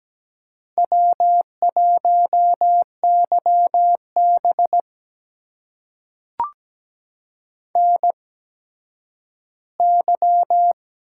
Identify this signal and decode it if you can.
Morse code, heard as a keyed tone, about 17 wpm, spaced as for 10 wpm, 700 Hz. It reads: W1YB NY
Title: USA Call Signs - 4 Characters - 8x Character Spacing - 17wpm